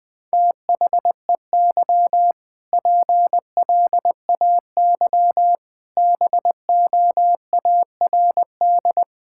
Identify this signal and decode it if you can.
Morse code, heard as a keyed tone, about 19 wpm, 700 Hz. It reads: THEY PLAY BOARD
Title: Difficult Sentences with Hints - 20wpm